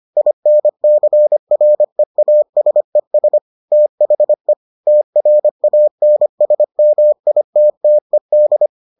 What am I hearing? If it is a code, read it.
Morse code, heard as a keyed tone, about 25 wpm, 600 Hz. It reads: INCREASES THE TRANSMITTED